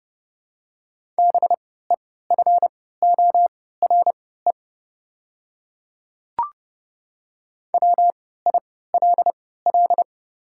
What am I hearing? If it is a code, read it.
Morse code, heard as a keyed tone, about 30 wpm, spaced as for 17 wpm, 700 Hz. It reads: BEFORE WILL
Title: 3x Character Spacing - Top 200 Words - 30wpm